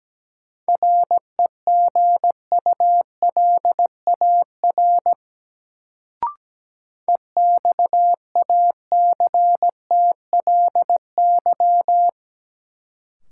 Morse code, heard as a keyed tone, about 17 wpm, 700 Hz. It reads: REGULAR EXACTLY
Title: Sentences from Top 900 Words - New Word Drill - 17wpm